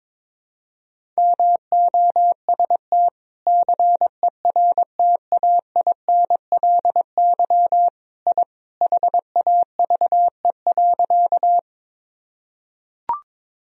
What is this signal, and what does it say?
Morse code, heard as a keyed tone, about 22 wpm, 700 Hz. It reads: MOST CERTAINLY I HAVE.
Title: Sentences from Top 1000 Words - Review Part 1 - 22wpm